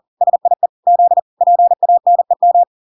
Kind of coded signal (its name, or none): Morse code